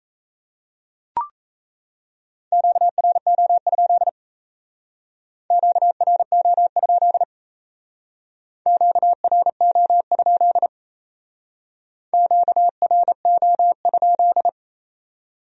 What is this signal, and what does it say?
Morse code, heard as a keyed tone, about 36 wpm, 700 Hz. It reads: QRO? QRO? QRO? QRO?